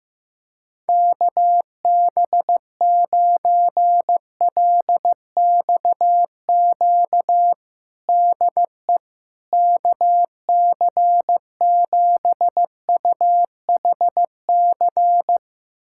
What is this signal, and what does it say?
Morse code, heard as a keyed tone, about 15 wpm, 700 Hz. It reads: KB9LXQ DE KC7UHC
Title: USA Callsign DE Callsign - 6 Characters - 15wpm